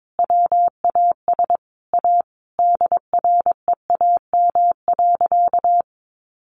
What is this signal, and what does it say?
Morse code, heard as a keyed tone, about 22 wpm, 700 Hz. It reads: WAS A DREAM.
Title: Sentences from Top 1500 Words 22wpm